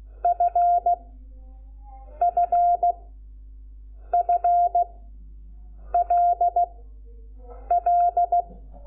Morse code, heard as a keyed tone, about 16 wpm, spaced as for 6 wpm, 700 Hz. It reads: FFFLL
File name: Lecciones / CLASE Nº5